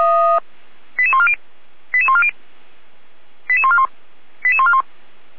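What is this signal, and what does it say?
Five-tone selective call (ZVEI), heard as quick runs of five tones, tones of about 70 ms. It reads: 88140 88140 88141 88141
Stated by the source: ZVEI five-tone selective call received on an SDR, hosted by Feuerwehr Orlamünde fire brigade.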